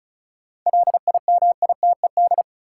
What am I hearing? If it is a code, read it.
Morse code, heard as a keyed tone, about 36 wpm, 700 Hz. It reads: LIMITED
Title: Sentences from Top 900 Words - New Word Drill - 35wpm